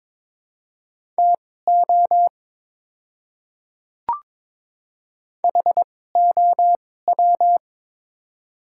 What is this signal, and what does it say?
Morse code, heard as a keyed tone, about 22 wpm, spaced as for 16 wpm, 690 Hz. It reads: TO HOW
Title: 2x Character Spacing - Top 100 Words - 22wpm